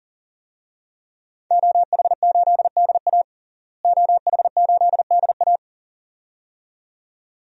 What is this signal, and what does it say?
Morse code, heard as a keyed tone, about 40 wpm, 700 Hz. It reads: OH8DA OH8DA
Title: DXCC-100 Call Signs - 5 Characters - Repeated 2x - 40wpm